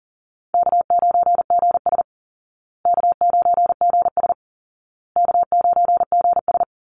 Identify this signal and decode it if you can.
Morse code, heard as a keyed tone, about 40 wpm, 700 Hz. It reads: K9GS K9GS K9GS
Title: Easy Call Signs Repeated 3x - 40wpm